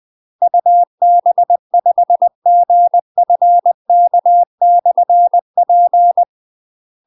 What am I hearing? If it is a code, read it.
Morse code, heard as a keyed tone, about 20 wpm, 700 Hz. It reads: UB5GFK/P
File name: UK MASTER QSOs TRAINING / 20WPM / UK MASTER QSOs 0068